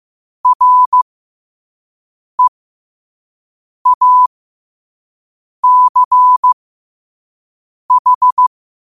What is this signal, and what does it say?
Morse code, heard as a keyed tone, about 15 wpm, spaced as for 5 wpm, 1000 Hz. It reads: REACH
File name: qotdfiles / 5wpm